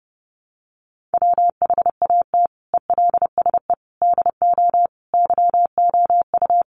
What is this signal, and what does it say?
Morse code, heard as a keyed tone, about 28 wpm, 700 Hz. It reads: WHAT ELSE DO YOU
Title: Sentences from Top 1100 Words 30wpm